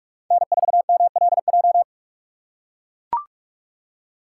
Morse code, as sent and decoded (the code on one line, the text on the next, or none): -. ....- -- .-.. .---
N4MLJ